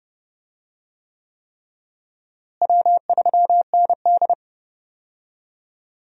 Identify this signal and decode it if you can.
Morse code, heard as a keyed tone, about 30 wpm, 700 Hz. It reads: W3ND
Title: USA Call Signs - 4 Characters - Repeated 3x - 30wpm